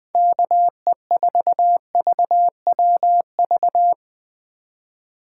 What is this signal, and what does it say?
Morse code, heard as a keyed tone, about 22 wpm, 700 Hz. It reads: KE4VWV
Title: Speed Racing - USA Call Signs 20wpm